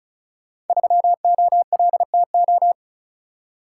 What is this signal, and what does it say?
Morse code, heard as a keyed tone, about 36 wpm, 700 Hz. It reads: 3OLTO
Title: Code Groups - 5 Letter:Numbers - 35wpm